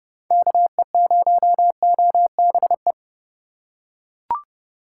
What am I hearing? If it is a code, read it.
Morse code, heard as a keyed tone, about 29 wpm, 700 Hz. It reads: KE0OBE